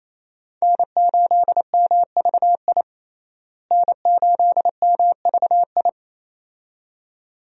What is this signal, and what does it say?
Morse code, heard as a keyed tone, about 28 wpm, 700 Hz. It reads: N8MVI N8MVI
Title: USA Call Signs - 5 Characters - Repeated 2x - 28wpm